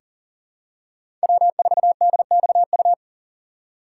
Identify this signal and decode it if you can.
Morse code, heard as a keyed tone, about 40 wpm, 700 Hz. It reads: W4DXU
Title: USA Call Signs - 5 Characters - 4x Character Spacing - 40wpm